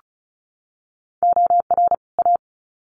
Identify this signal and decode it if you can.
Morse code, heard as a keyed tone, about 34 wpm, 700 Hz. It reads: OR A
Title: Sets of 2 Words 35wpm